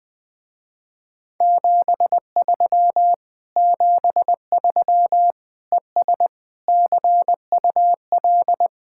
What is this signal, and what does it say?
Morse code, heard as a keyed tone, about 21 wpm, 700 Hz. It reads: 73 73 ES CUL
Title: Speed-Racing Common QSO Phrases Encore - 20wpm